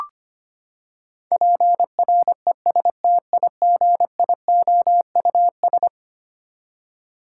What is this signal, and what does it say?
Morse code, heard as a keyed tone, about 25 wpm, 700 Hz. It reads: PRESTIGIOUS
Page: morse-code-ninja-practice-25wpm